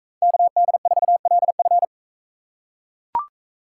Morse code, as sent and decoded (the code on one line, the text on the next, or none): -.- -.. ....- .-.. ..-.
KD4LF